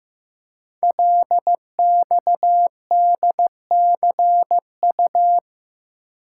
Morse code, heard as a keyed tone, about 15 wpm, 700 Hz. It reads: LXDCU
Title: Code Groups - 5 Letters - 15wpm